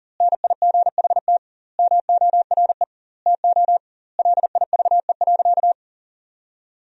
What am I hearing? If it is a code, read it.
Morse code, heard as a keyed tone, about 40 wpm, 700 Hz. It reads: NIGHT MORE TO LIVE.
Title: 2x Word-Spacing - Sentences from Top 300 Words - 40wpm